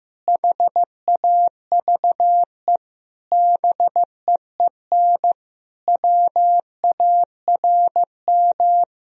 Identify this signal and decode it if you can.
Morse code, heard as a keyed tone, about 15 wpm, 700 Hz. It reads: HAVE BEEN WARM
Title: Sentences from Top 1000 Words - Review Part 2 - 15wpm